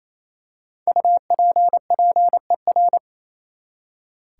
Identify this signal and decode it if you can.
Morse code, heard as a keyed tone, about 28 wpm, 700 Hz. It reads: UPPER
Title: Five Letter Words - 28wpm